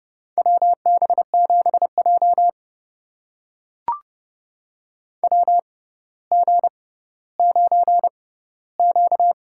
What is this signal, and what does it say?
Morse code, heard as a keyed tone, about 30 wpm, spaced as for 15 wpm, 690 Hz. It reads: WB7J WG9Q
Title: USA Call Signs - 4 Characters - 6x Character Spacing - 30wpm